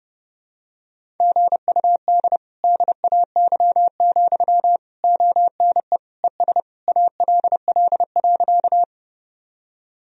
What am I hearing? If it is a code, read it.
Morse code, heard as a keyed tone, about 30 wpm, 700 Hz. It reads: GUD DAY, ONE ES ALL.
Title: Sentences from Top 200 Words and Abbreviations 30wpm